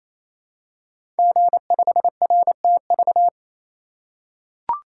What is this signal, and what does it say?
Morse code, heard as a keyed tone, about 28 wpm, 700 Hz. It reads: G5RTV
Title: Code Groups - 5 Letter:Numbers - 28wpm